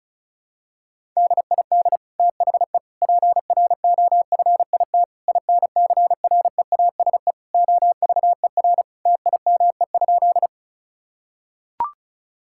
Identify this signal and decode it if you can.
Morse code, heard as a keyed tone, about 34 wpm, 700 Hz. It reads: DID THE PROFIT INCREASE OVER TIME?
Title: Sentences from Top 1000 Words - Review Part 5 - 35wpm